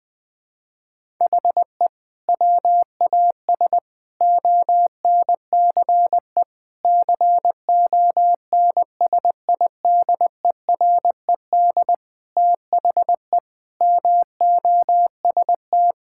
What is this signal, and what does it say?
Morse code, heard as a keyed tone, about 20 wpm, 700 Hz. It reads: HE WAS ONCE CONSIDERED THE MOST
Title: Sentences with 7-Word Phrases - Spoken Beginning - 20wpm